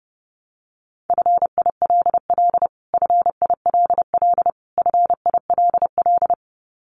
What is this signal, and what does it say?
Morse code, heard as a keyed tone, about 28 wpm, 700 Hz. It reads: FILL FILL FILL
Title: Top 500 Words Repeated 30wpm